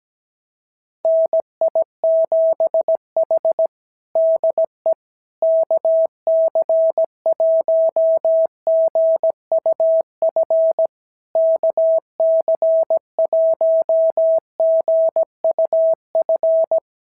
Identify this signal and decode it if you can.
Morse code, heard as a keyed tone, about 17 wpm, 650 Hz. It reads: NI7H DE KC1GUF KC1GUF
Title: SOTA QSO Practice 17wpm